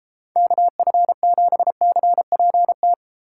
Morse code, heard as a keyed tone, about 33 wpm, 700 Hz. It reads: KF7CPT